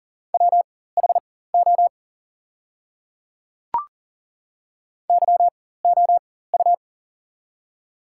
Morse code, heard as a keyed tone, about 40 wpm, spaced as for 19 wpm, 690 Hz. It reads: WHO YOU
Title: 4x Character Spacing - Top 100 Words - 40wpm